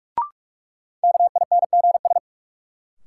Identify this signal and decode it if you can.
Morse code, heard as a keyed tone, about 46 wpm, 700 Hz. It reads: KINGS
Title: Sentences from Top 1000 Words - Review Part 6 - New Word Drill - 45wpm